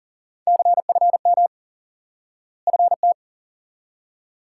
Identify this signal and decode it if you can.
Morse code, heard as a keyed tone, about 37 wpm, 700 Hz. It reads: CFM FT
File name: IQE Course 3 - Lesson 89 - YAGI - Two Elements - 40wpm